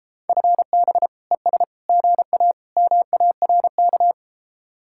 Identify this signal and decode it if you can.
Morse code, heard as a keyed tone, about 33 wpm, 700 Hz. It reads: FB ES GA MARK